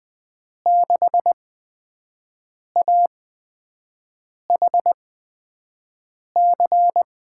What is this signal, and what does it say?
Morse code, heard as a keyed tone, about 20 wpm, spaced as for 5 wpm, 700 Hz. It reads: 6AHC
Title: USA Call Signs - 6 Characters - 8x Character Spacing - 20wpm